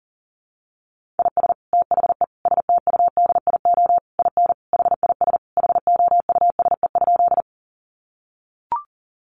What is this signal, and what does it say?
Morse code, heard as a keyed tone, about 40 wpm, 700 Hz. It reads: IS THE STUDIO IN HIS HOUSE?
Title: Sentences from Top 1200 Words 40wpm